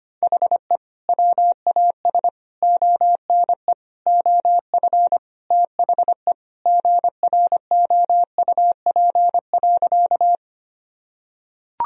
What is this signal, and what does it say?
Morse code, heard as a keyed tone, about 25 wpm, 700 Hz. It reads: HE WAS ONE OF THE GROUP.
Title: Sentences from Top 1000 Words - Review Part 1 - 25wpm